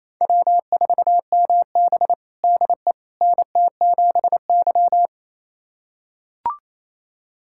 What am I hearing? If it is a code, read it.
Morse code, heard as a keyed tone, about 27 wpm, 700 Hz. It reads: W4MB DE NT7Y
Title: USA Callsign DE Callsign - 4 Characters - 28wpm